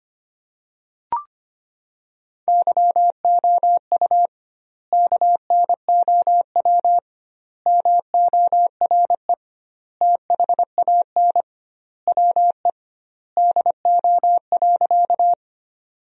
Morse code, heard as a keyed tone, about 25 wpm, 700 Hz. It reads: YOU KNOW MORE THAN WE DO.
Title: 2x Word-Spacing - Sentences from Top 100 Words - 25wpm